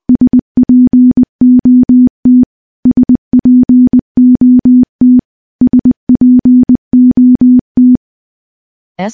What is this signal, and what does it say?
Morse code, heard as a keyed tone, about 20 wpm, 250 Hz. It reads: SPOT SPOT SPOT
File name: morse-training-20WPM